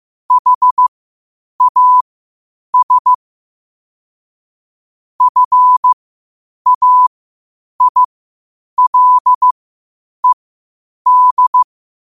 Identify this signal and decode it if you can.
Morse code, heard as a keyed tone, about 15 wpm, spaced as for 8 wpm, 1000 Hz. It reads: HAS FAILED
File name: qotdfiles / 8wpm